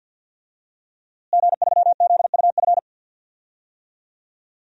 Morse code, heard as a keyed tone, about 53 wpm, 700 Hz. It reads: G3ZUF
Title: DXCC-100 Call Signs - 5 Characters - Repeated 2x - 50wpm